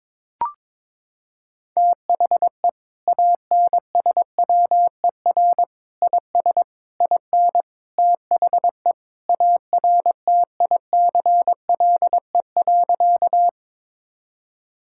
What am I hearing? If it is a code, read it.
Morse code, heard as a keyed tone, about 22 wpm, 700 Hz. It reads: THE ANSWER IS IN THE ARTICLE.
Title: Sentences from Top 1000 Words - Review Part 5 - 22wpm